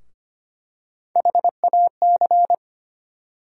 Morse code, heard as a keyed tone, about 25 wpm, 700 Hz. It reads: HAC
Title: USA Call Sign Postfixes - 3 Letters 25wpm